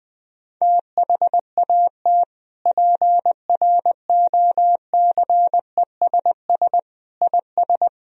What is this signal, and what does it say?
Morse code, heard as a keyed tone, about 20 wpm, 700 Hz. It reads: THAT PROCESS IS
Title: Sentences from Top 1000 Words - Review Part 3 - 20wpm